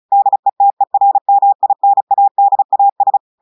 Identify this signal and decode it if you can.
Morse code, heard as a keyed tone, about 36 wpm, 800 Hz. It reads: DETERMINADAS
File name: portugais texte en clair 2 vitesse 35 mots minutes